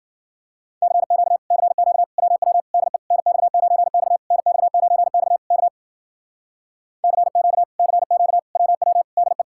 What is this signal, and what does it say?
Morse code, heard as a keyed tone, about 53 wpm, 700 Hz. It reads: CQ CQ WW DE NX9X NX9X K CQ CQ WW DE